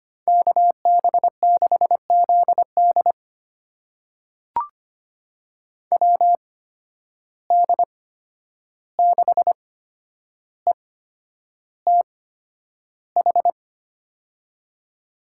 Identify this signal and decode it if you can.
Morse code, heard as a keyed tone, about 25 wpm, spaced as for 10 wpm, 700 Hz. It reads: KB6ZD WD6ETH
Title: USA Call Signs - 8x Character Spacing - 25wpm